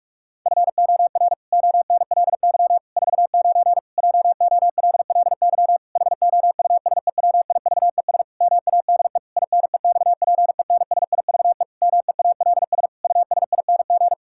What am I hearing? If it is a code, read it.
Morse code, heard as a keyed tone, about 46 wpm, 700 Hz. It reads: FOR ONLY 49 JOLLY HOUSEWIVES MADE INEXPENSIVE MEALS USING
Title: Pangram Challenge v2 - 45wpm